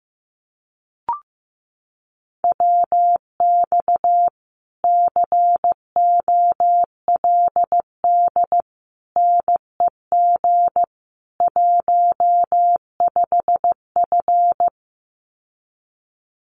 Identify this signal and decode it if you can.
Morse code, heard as a keyed tone, about 15 wpm, 700 Hz. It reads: WX COLD NEG 15F
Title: Common QSO Phrases Encore 15wpm